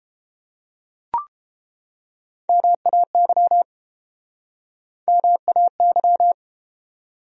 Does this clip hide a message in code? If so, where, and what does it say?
Yes: Morse code from 2.5 to 6.3 s, heard as a keyed tone, about 32 wpm, 700 Hz. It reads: MAY MAY